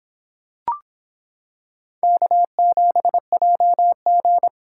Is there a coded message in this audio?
Yes: Morse code from 2.0 to 4.5 s, heard as a keyed tone, about 26 wpm, 700 Hz. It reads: K7JG